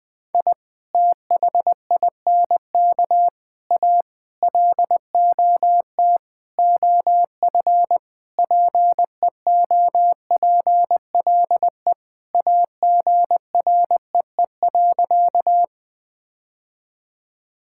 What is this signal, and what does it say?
Morse code, heard as a keyed tone, about 20 wpm, 700 Hz. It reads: I THINK A LOT OF PEOPLE AGREE.
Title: Sentences with 5-Word Phrases - Spoken Beginning - 20wpm